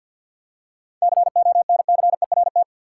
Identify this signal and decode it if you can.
Morse code, heard as a keyed tone, about 51 wpm, 700 Hz. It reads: CONCERT
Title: Seven Letter Words - 50wpm